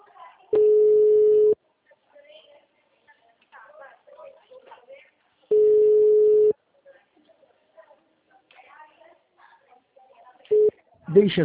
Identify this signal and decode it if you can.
Ringback tone (Europe, 425 Hz), with speech.